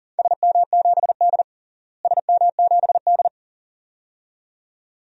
Morse code, heard as a keyed tone, about 37 wpm, 700 Hz. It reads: SM7D SM7D
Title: DXCC-100 Call Signs - 4 Characters - Repeated 2x - 40wpm